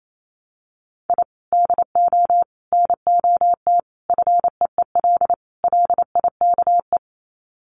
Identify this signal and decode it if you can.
Morse code, heard as a keyed tone, about 28 wpm, 700 Hz. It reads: I DO NOT FEEL LIKE
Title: Sets of 5 Words Encore 28wpm